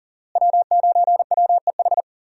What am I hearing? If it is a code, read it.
Morse code, heard as a keyed tone, about 39 wpm, 700 Hz. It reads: W9WEH
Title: USA Call Signs - 5 Characters - Repeated 2x - 40wpm